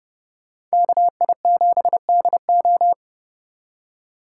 Morse code, heard as a keyed tone, about 38 wpm, 700 Hz. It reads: KI7DO